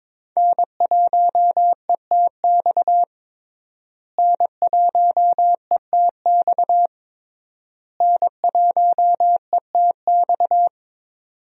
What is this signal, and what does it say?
Morse code, heard as a keyed tone, about 21 wpm, 700 Hz. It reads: N1ETX N1ETX N1ETX
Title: USA Call Signs - 5 Characters - Repeated 3x - 22wpm